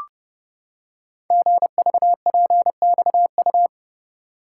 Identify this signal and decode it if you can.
Morse code, heard as a keyed tone, about 30 wpm, 700 Hz. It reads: GVPXU
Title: Code Groups - 5 Letters - 30wpm